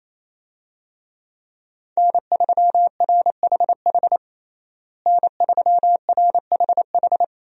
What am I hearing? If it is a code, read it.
Morse code, heard as a keyed tone, about 28 wpm, 700 Hz. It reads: N3RHH N3RHH
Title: USA Call Signs - 5 Characters - Repeated 2x - 28wpm